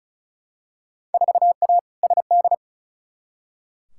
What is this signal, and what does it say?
Morse code, heard as a keyed tone, about 36 wpm, 700 Hz. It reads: 4A SD
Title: Field Day Exchanges - 35wpm